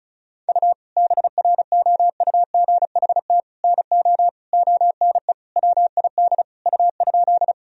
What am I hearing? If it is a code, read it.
Morse code, heard as a keyed tone, about 36 wpm, 700 Hz. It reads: U BROUGHT NO ONE WID U?
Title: Sentences from Top 500 Words and Abbreviations 35wpm